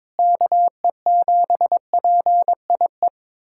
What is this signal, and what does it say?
Morse code, heard as a keyed tone, about 22 wpm, 700 Hz. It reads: KE7PIE
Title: USA Callsign DE Callsign - 6 Characters - 22wpm